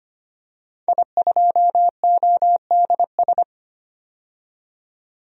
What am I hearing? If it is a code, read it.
Morse code, heard as a keyed tone, about 25 wpm, 700 Hz. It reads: I2ODS
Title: DXCC-100 Call Signs - 5 Characters - 25wpm